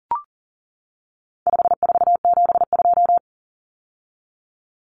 Morse code, heard as a keyed tone, about 40 wpm, 700 Hz. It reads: 5472
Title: Common Cut Numbers - 4-Digit 40wpm